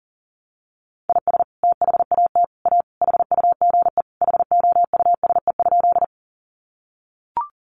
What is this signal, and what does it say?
Morse code, heard as a keyed tone, about 37 wpm, 700 Hz. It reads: IS THAT A HUGE HOUSE?
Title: Sentences from Top 1100 Words 40wpm